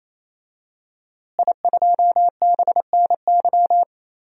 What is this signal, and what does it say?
Morse code, heard as a keyed tone, about 28 wpm, 700 Hz. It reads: I2BNY
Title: DXCC-100 Call Signs - 5 Characters - Repeated 3x - 28wpm